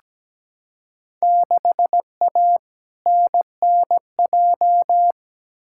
Morse code, heard as a keyed tone, about 17 wpm, 700 Hz. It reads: 6A NNJ
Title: Field Day Exchanges Encore - 17wpm